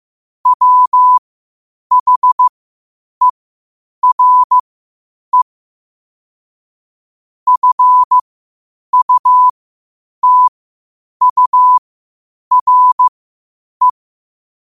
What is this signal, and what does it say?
Morse code, heard as a keyed tone, about 15 wpm, spaced as for 8 wpm, 1000 Hz. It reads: WHERE FUTURE
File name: qotdfiles / 8wpm